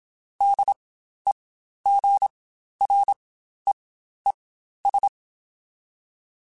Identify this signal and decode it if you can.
Morse code, heard as a keyed tone, about 27 wpm, spaced as for 12 wpm, 800 Hz. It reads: DEGREES
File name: Spaced Characters / Spaced Characters-mp3s / Weather4 21 sp @25